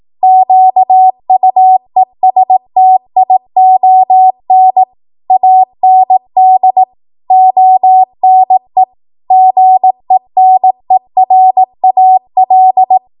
Morse code, heard as a keyed tone, about 18 wpm, 750 Hz. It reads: QUESTION AND ONE GENERAL